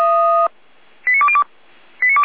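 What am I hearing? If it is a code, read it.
Five-tone selective call (ZVEI), heard as quick runs of five tones, tones of about 70 ms. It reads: 89202
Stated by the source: ZVEI five-tone call received on SDR, feuerwehr-orlamuende.org (fire brigade Orlamünde).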